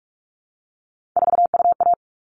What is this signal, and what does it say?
Morse code, heard as a keyed tone, about 44 wpm, 700 Hz. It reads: 4UA
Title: All Cut Numbers - 3-Digit 45wpm